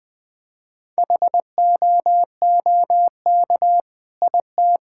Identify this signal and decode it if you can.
Morse code, heard as a keyed tone, about 20 wpm, 700 Hz. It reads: HOOK IT
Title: Sentences from Top 1600 Words 20wpm